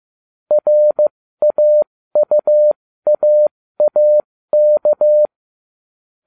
Morse code, heard as a keyed tone, about 15 wpm, spaced as for 13 wpm, 600 Hz. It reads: RAUAAK